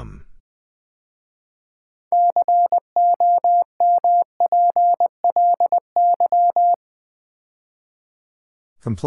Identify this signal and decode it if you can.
Morse code, heard as a keyed tone, about 20 wpm, 700 Hz. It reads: COMPLY
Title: 6 Letter Words - First 3 Letters Given - 20wpm